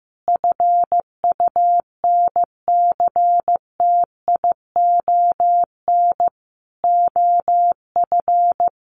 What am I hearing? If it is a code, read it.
Morse code, heard as a keyed tone, about 15 wpm, 700 Hz. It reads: FUNCTION OF
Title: Sentences from Top 1100 Words 15wpm